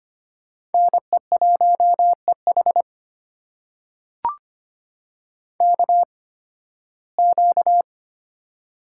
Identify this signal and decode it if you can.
Morse code, heard as a keyed tone, about 25 wpm, spaced as for 16 wpm, 700 Hz. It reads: NE1EH KQ
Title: USA Call Signs - 5 Characters - 8x Character Spacing - 25wpm